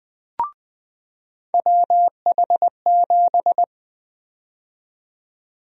Morse code, heard as a keyed tone, about 20 wpm, 700 Hz. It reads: WH7
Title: USA Call Sign Prefixes - 20wpm